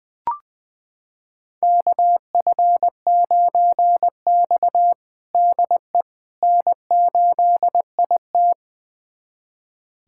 Morse code, heard as a keyed tone, about 20 wpm, 700 Hz. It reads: KF9X DE N8IT
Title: USA Callsign DE Callsign - 4 Characters - 20wpm